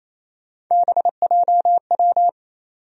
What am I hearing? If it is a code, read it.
Morse code, heard as a keyed tone, about 28 wpm, 700 Hz. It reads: BJW
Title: USA Call Sign Postfixes - 3 Letters 28wpm